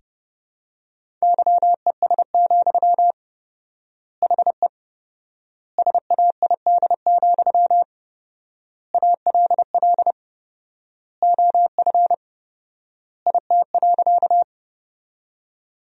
Morse code, heard as a keyed tone, about 28 wpm, 700 Hz. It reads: YES, HE SAID, ALL OF IT.
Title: 4x Word-Spacing - Sentences from Top 500 Words - 30wpm